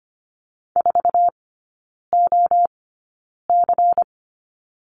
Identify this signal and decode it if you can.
Morse code, heard as a keyed tone, about 25 wpm, spaced as for 9 wpm, 700 Hz. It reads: 4OC
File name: course / 25:10wpm / 093 Lesson 16e - Letter c - Three Characters 25wpm